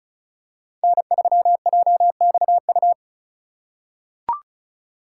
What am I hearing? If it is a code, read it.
Morse code, heard as a keyed tone, about 36 wpm, 690 Hz. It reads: N3JXU
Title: USA Call Signs - 5 Characters - 3x Character Spacing - 35wpm